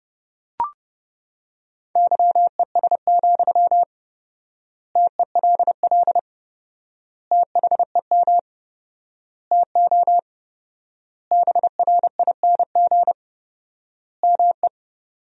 Morse code, heard as a keyed tone, about 30 wpm, 700 Hz. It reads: YES, TELL THEM TO BRING ME
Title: 4x Word-Spacing - Sentences from Top 500 Words - 30wpm